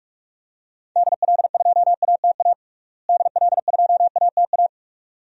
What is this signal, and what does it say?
Morse code, heard as a keyed tone, about 43 wpm, 700 Hz. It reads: DL2ATA DL2ATA
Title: DXCC-100 Call Signs - 6 Characters - Repeated 2x - 45wpm